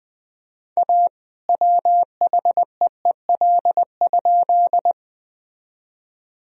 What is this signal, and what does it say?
Morse code, heard as a keyed tone, about 20 wpm, 700 Hz. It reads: A WHEEL?
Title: Sentences from Top 1000 Words - Review Part 3 - 20wpm